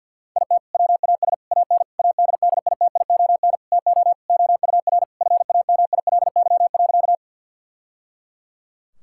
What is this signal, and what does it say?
Morse code, heard as a keyed tone, about 50 wpm, 700 Hz. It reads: IT WAS AN ADDITION TO OUR FAMILY.
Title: Sentences from Top 1000 Words - Review Part 6 - 50wpm